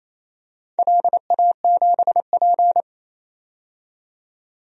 Morse code, heard as a keyed tone, about 28 wpm, 700 Hz. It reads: LA7P
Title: DXCC-100 Call Signs - 4 Characters - 28wpm